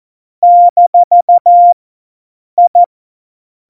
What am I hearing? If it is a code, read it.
Morse code, heard as a keyed tone, about 14 wpm, 700 Hz. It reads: - I
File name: Morsekurs / Witze